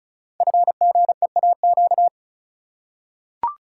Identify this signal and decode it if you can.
Morse code, heard as a keyed tone, about 36 wpm, 700 Hz. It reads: FGEAQ
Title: Code Groups - 5 Letters - 35wpm